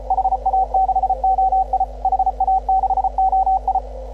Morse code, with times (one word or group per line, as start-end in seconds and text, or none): 0.1-1.8 HA6OI
2.0-3.8 HA6OI